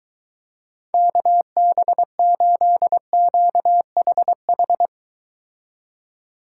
Morse code, heard as a keyed tone, about 24 wpm, 700 Hz. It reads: KB8QHH